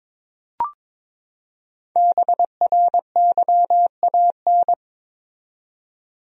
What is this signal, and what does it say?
Morse code, heard as a keyed tone, about 22 wpm, 700 Hz. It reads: BRYAN
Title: USA Common Names - Max 5 Letters - 22wpm